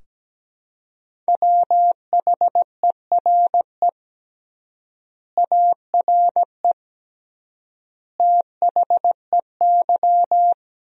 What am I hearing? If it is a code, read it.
Morse code, heard as a keyed tone, about 17 wpm, 700 Hz. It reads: WHERE ARE THEY
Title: 3x Word-Spacing - Sentences from Top 200 Words - 17wpm